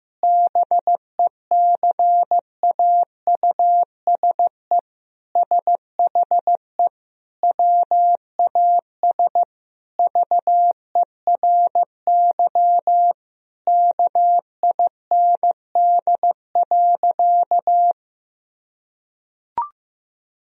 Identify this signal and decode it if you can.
Morse code, heard as a keyed tone, about 15 wpm, 700 Hz. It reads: BECAUSE SHE WAS VERY KIND.
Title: Sentences from Top 1000 Words - Review Part 3 - 15wpm